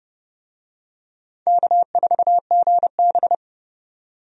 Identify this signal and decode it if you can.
Morse code, heard as a keyed tone, about 30 wpm, 700 Hz. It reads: K4GB
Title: USA Call Signs - 4 Characters - 30wpm